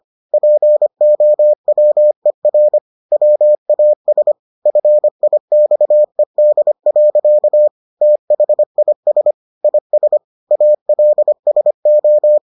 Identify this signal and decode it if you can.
Morse code, heard as a keyed tone, about 25 wpm, 600 Hz. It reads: POWER WAS FIXED. THIS IS ALSO